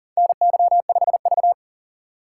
Morse code, heard as a keyed tone, about 38 wpm, 700 Hz. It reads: NY5V
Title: USA Call Signs - 4 Characters - 3x Character Spacing - 40wpm